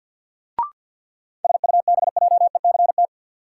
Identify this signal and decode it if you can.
Morse code, heard as a keyed tone, about 52 wpm, 700 Hz. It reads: SUBJECT
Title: Sentences from Top 800 Words - New Word Drill - 50wpm